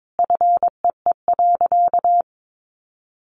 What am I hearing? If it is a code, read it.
Morse code, heard as a keyed tone, about 22 wpm, 700 Hz. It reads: FEE.
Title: Sentences from Top 1500 Words 22wpm